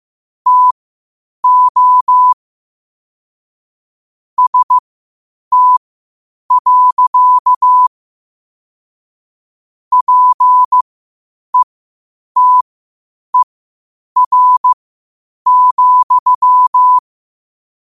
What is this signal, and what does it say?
Morse code, heard as a keyed tone, about 15 wpm, spaced as for 8 wpm, 1000 Hz. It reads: TO ST. PETER,
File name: qotdfiles / 8wpm